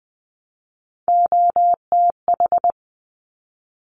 Morse code, heard as a keyed tone, about 20 wpm, 700 Hz. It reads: OTH
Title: Top Three Letter Combinations in Words 20wpm